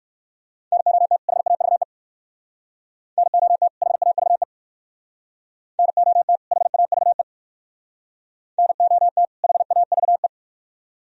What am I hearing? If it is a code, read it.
Morse code, heard as a keyed tone, about 59 wpm, 700 Hz. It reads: NOT HAVE NOT HAVE NOT HAVE NOT HAVE